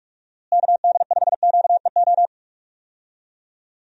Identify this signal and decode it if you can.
Morse code, heard as a keyed tone, about 46 wpm, 700 Hz. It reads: KD5QEO
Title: USA Call Signs - 6 Characters - Repeated 3x - 45wpm